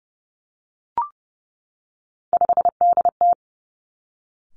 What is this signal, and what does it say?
Morse code, heard as a keyed tone, about 30 wpm, 700 Hz. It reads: 5DT
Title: Three Challenging Letter-Number Combinations 30wpm